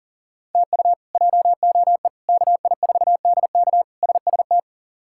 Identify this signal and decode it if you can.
Morse code, heard as a keyed tone, about 37 wpm, 700 Hz. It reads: TU JOE KI4DK SST
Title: K1USN SST QSO Practice - 40wpm